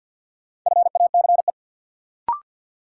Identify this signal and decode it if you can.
Morse code, heard as a keyed tone, about 53 wpm, 700 Hz. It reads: FACE
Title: Sentences from Top 300 Words - New Word Drill - 50wpm